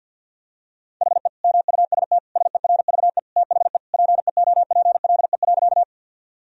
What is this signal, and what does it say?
Morse code, heard as a keyed tone, about 50 wpm, 700 Hz. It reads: HE MUST SERVE THE PEOPLE.
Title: Sentences from Top 1000 Words - Review Part 2 - 50wpm